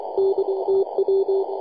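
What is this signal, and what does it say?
Morse code, heard as a keyed tone, about 23 wpm, 390 Hz, with radio noise. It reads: DTW
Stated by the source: YJ0GA on 18.077 MHz CW, recorded by ja0rug.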